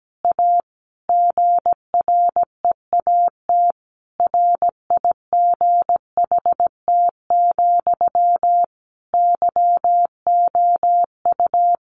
Morse code, heard as a keyed tone, about 17 wpm, 700 Hz. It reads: A GREAT RIGHT, YOU
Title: Sentences from Top 900 Words 17wpm